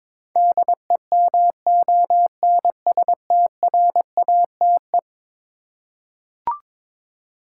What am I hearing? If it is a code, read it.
Morse code, heard as a keyed tone, about 22 wpm, 700 Hz. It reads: DEMONSTRATE
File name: practice / 22wpm / Eleven Letter Words - 22wpm